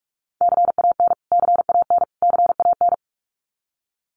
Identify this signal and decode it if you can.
Morse code, heard as a keyed tone, about 44 wpm, 700 Hz. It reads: CAN CAN CAN
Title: Top 500 Words Repeated 45wpm